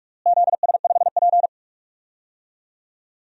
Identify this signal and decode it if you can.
Morse code, heard as a keyed tone, about 46 wpm, 700 Hz. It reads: ZS5P
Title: DXCC-100 Call Signs - 4 Characters - Repeated 2x - 45wpm